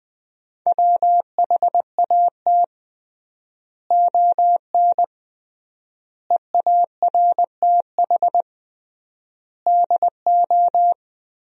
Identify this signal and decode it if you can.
Morse code, heard as a keyed tone, about 20 wpm, 700 Hz. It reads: WHAT ON EARTH DO
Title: 3x Word-Spacing - Sentences from Top 200 Words - 20wpm